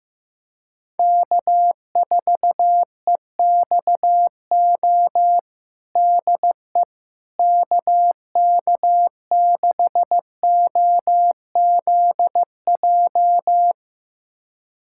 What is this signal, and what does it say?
Morse code, heard as a keyed tone, about 15 wpm, 700 Hz. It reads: K4EXO DE KK6OZJ
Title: USA Callsign DE Callsign 15wpm